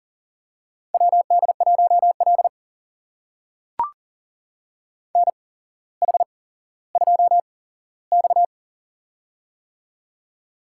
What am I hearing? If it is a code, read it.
Morse code, heard as a keyed tone, about 42 wpm, spaced as for 17 wpm, 700 Hz. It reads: WD1L NH2X
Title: USA Call Signs - 4 Characters - 8x Character Spacing - 40wpm